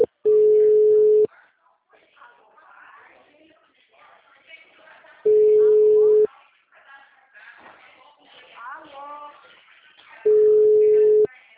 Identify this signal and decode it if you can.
Ringback tone (Europe, 425 Hz), with speech.